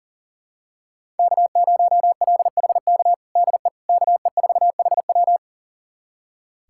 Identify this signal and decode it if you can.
Morse code, heard as a keyed tone, about 40 wpm, 700 Hz. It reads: K0LHK DE KE4HW
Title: USA Callsign DE Callsign - 5 Characters - 40wpm